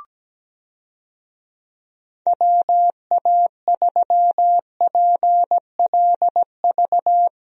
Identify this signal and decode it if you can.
Morse code, heard as a keyed tone, about 17 wpm, 700 Hz. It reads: WA3PLV
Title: USA Call Signs - 6 Characters - Repeated 2x - 17wpm